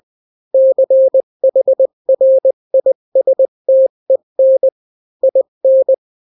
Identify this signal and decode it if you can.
Morse code, heard as a keyed tone, about 20 wpm, 530 Hz. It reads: CHRISTEN IN
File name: daily / 18wpm